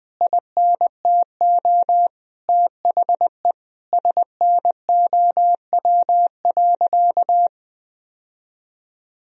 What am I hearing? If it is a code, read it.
Morse code, heard as a keyed tone, about 21 wpm, 700 Hz. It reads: INTO THE SNOW.
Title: Sentences from Top 1000 Words - Review Part 3 - 20wpm